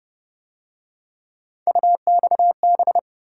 Morse code, heard as a keyed tone, about 29 wpm, 700 Hz. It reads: UXB
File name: Speed-Racing - Three Letter-Number Combinations - 25wpm